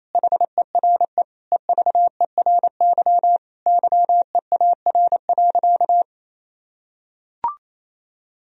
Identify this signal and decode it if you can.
Morse code, heard as a keyed tone, about 28 wpm, 700 Hz. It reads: HERE EVERY YEAR.
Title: Sentences from Top 1000 Words - Review Part 1 - 28wpm